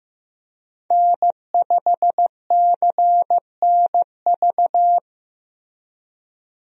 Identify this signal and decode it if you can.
Morse code, heard as a keyed tone, about 15 wpm, 700 Hz. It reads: N5CNV
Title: USA Call Signs - 5 Characters - Repeated 3x - 15wpm